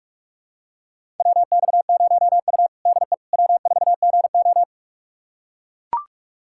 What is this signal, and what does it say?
Morse code, heard as a keyed tone, about 41 wpm, 700 Hz. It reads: WX0U DE W4GO
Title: USA Callsign DE Callsign - 4 Characters - 45wpm